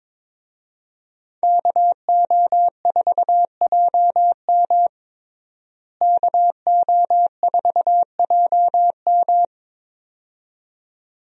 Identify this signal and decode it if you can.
Morse code, heard as a keyed tone, about 22 wpm, 700 Hz. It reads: KO4JM KO4JM
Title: USA Call Signs - 5 Characters - Repeated 2x - 22wpm